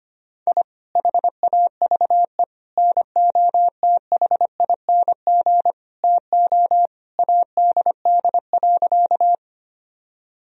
Morse code, heard as a keyed tone, about 25 wpm, 700 Hz. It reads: I HAVE NOTHING TO ADD.
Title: Sentences from Top 1000 Words - Review Part 1 - 25wpm